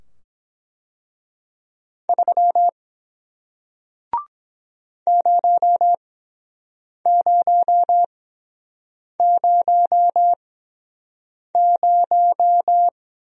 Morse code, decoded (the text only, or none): 3 0000